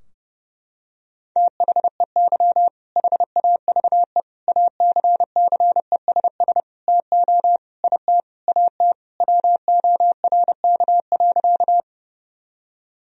Morse code, heard as a keyed tone, about 32 wpm, 700 Hz. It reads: THEY HAVE ACCESS TO IT AT WORK.
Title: Sentences from Top 1000 Words - Review Part 4 - 30wpm